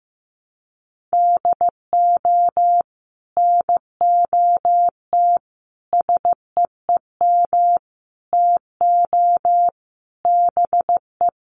Morse code, heard as a keyed tone, about 15 wpm, 700 Hz. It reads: DO NOT SEEM TO BE
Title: Sets of 5 Words Encore2 15wpm